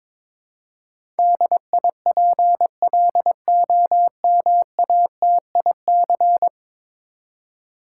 Morse code, heard as a keyed tone, about 22 wpm, 700 Hz. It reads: DIPLOMATIC
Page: morse-code-ninja-practice-22wpm